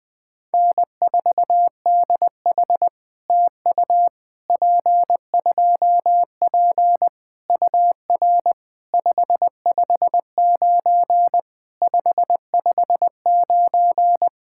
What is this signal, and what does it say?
Morse code, heard as a keyed tone, about 20 wpm, 700 Hz. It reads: N4DH TU P2P UR 559 559
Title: POTA QSO Practice - 20wpm